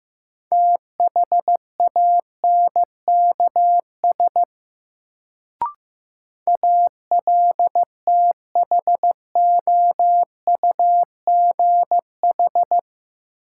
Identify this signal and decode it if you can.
Morse code, heard as a keyed tone, about 15 wpm, 700 Hz. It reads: THANKS ALTHOUGH
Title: Sentences from Top 600 Words - New Word Drill - 15wpm